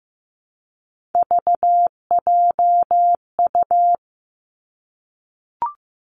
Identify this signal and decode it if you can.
Morse code, heard as a keyed tone, about 15 wpm, 700 Hz. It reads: VJU
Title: Three Challenging Letter-Number Combinations 15wpm